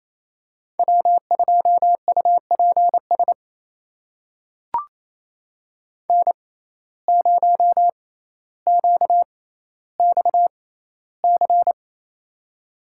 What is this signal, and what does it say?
Morse code, heard as a keyed tone, about 28 wpm, spaced as for 14 wpm, 700 Hz. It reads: W2UPS N0QXC
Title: USA Call Signs - 5 Characters - 6x Character Spacing - 28wpm